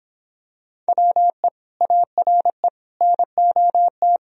Morse code, heard as a keyed tone, about 26 wpm, 700 Hz. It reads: WE ARE NOT